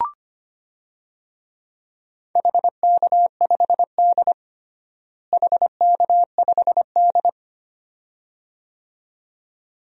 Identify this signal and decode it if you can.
Morse code, heard as a keyed tone, about 25 wpm, 700 Hz. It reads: HK5D HK5D
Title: DXCC-100 Call Signs - 4 Characters - Repeated 2x - 25wpm